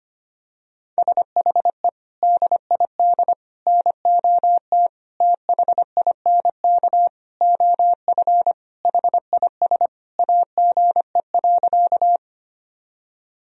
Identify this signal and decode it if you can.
Morse code, heard as a keyed tone, about 25 wpm, 700 Hz. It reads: SHE DID NOT THINK OF HIS AGE.
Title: Sentences from Top 1000 Words - Review Part 3 - 25wpm